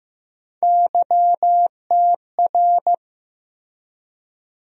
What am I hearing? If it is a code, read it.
Morse code, heard as a keyed tone, about 15 wpm, 700 Hz. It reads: YTR
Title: USA Call Sign Postfixes - 3 Letters 15wpm